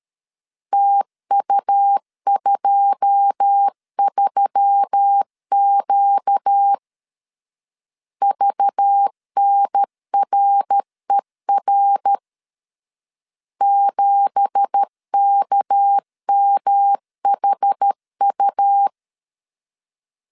Morse code, heard as a keyed tone, about 13 wpm, 800 Hz. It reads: TU23Q VNRER 7KMHU
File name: CW / DET / 1 9 1500C S C 12wpm 3811